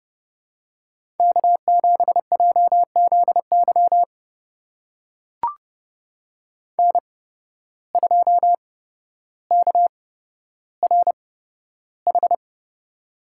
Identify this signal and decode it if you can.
Morse code, heard as a keyed tone, about 32 wpm, spaced as for 13 wpm, 700 Hz. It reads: K7JZY N2KRH
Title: USA Call Signs - 5 Characters - 8x Character Spacing - 30wpm